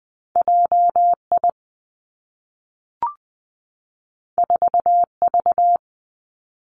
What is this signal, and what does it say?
Morse code, heard as a keyed tone, about 20 wpm, 700 Hz. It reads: J I 4 V
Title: Two Challenging Letter-Number Combinations 20wpm